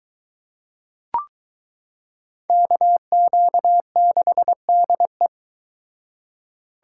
Morse code, heard as a keyed tone, about 23 wpm, 700 Hz. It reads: KQ6DE